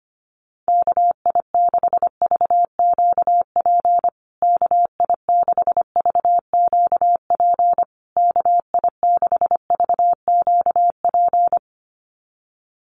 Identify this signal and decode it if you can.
Morse code, heard as a keyed tone, about 25 wpm, 700 Hz. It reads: KI6VQP KI6VQP KI6VQP
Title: USA Call Signs Repeated 3x 25wpm